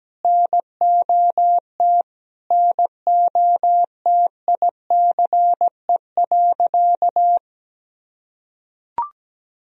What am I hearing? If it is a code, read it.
Morse code, heard as a keyed tone, about 17 wpm, 700 Hz. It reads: NOT NOTICE.
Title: Sentences from Top 1000 Words - Review Part 2 - 17wpm